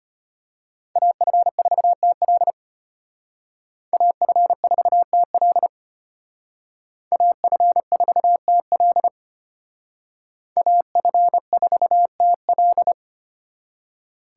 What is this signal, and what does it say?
Morse code, heard as a keyed tone, about 32 wpm, 700 Hz. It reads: AF4TL AF4TL AF4TL AF4TL